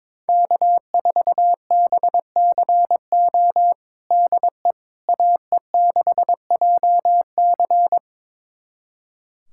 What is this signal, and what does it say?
Morse code, heard as a keyed tone, about 22 wpm, 700 Hz. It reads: K4BCO DE AE6JC
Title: USA Callsign DE Callsign - 5 Characters - 22wpm